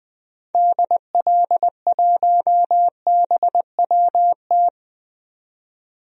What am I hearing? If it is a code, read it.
Morse code, heard as a keyed tone, about 20 wpm, 700 Hz. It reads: DL1BWT
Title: DXCC-100 Call Signs - 6 Characters - 20wpm